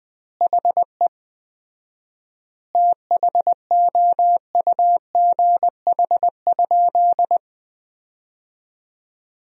Morse code, heard as a keyed tone, about 20 wpm, 700 Hz. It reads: HE THOUGH?
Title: 4x Word-Spacing - Sentences from Top 400 Words - 20wpm